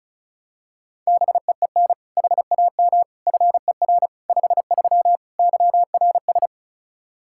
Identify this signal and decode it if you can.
Morse code, heard as a keyed tone, about 36 wpm, 700 Hz. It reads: BEEN HAM FER 53 YRS
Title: Practice QSO - Exchange 3 - More Info - 35wpm